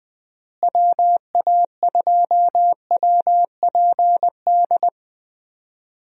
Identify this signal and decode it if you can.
Morse code, heard as a keyed tone, about 20 wpm, 700 Hz. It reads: WA2WPD
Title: USA Call Signs - 6 Characters - Repeated 2x - 20wpm